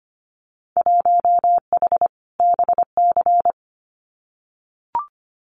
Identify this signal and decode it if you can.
Morse code, heard as a keyed tone, about 25 wpm, 690 Hz. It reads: 1H BC
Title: Winter Field Day Exchanges 25wpm